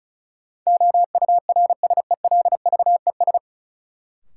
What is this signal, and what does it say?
Morse code, heard as a keyed tone, about 34 wpm, 700 Hz. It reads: OURSELVES
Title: Sentences from Top 1600 Words - New Word Drill - 35wpm